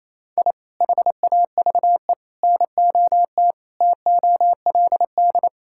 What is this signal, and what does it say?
Morse code, heard as a keyed tone, about 28 wpm, 700 Hz. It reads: I HAVE NOT TOLD
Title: Sentences from Top 1000 Words - Review Part 5 - 28wpm